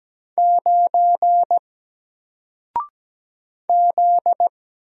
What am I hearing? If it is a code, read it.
Morse code, heard as a keyed tone, about 17 wpm, 700 Hz. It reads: 9 Z
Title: Single Letter-Number 17wpm